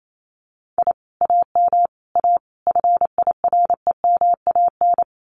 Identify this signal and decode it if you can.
Morse code, heard as a keyed tone, about 28 wpm, 700 Hz. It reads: I AM A FIREMAN
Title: Common QSO Phrases Encore 28wpm